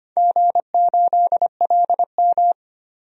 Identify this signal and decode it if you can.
Morse code, heard as a keyed tone, about 25 wpm, 700 Hz. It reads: G8LM
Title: DXCC-100 Call Signs - 4 Characters - 25wpm